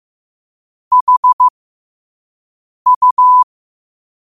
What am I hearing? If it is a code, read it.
Morse code, heard as a keyed tone, about 15 wpm, spaced as for 5 wpm, 1000 Hz. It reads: HU
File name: qotdfiles / 5wpm